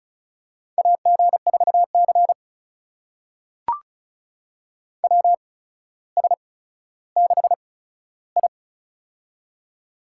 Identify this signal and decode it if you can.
Morse code, heard as a keyed tone, about 36 wpm, spaced as for 15 wpm, 700 Hz. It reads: AG4C WS6I
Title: USA Call Signs - 4 Characters - 8x Character Spacing - 35wpm